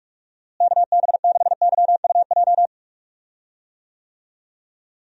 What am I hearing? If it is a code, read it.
Morse code, heard as a keyed tone, about 42 wpm, 700 Hz. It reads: KB6YUJ
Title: USA Call Signs - 6 Characters - Repeated 2x - 45wpm